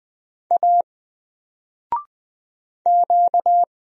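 Morse code, heard as a keyed tone, about 20 wpm, 700 Hz. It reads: A Q
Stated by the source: Single letter and number Morse drill at 20 wpm, from N4LCD.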